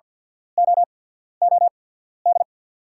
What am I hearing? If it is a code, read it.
Morse code, heard as a keyed tone, about 48 wpm, spaced as for 13 wpm, 690 Hz. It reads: OOD
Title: 8x Character Spacing - Top 200 Words - 50wpm